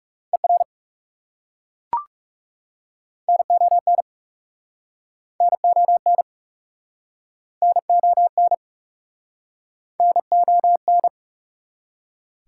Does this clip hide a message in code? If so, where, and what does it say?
Yes: Morse code from 0.3 to 11.1 s, heard as a keyed tone, about 42 wpm, 700 Hz. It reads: ER NON NON NON NON